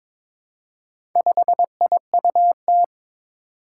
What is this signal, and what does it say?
Morse code, heard as a keyed tone, about 22 wpm, 700 Hz. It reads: 5IUT
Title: Four Challenging Letter-Number Combinations 22wpm